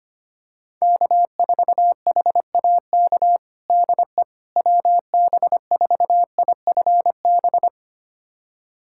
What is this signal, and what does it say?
Morse code, heard as a keyed tone, about 25 wpm, 700 Hz. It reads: K4HAK DE WB4IFB
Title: USA Callsign DE Callsign 25wpm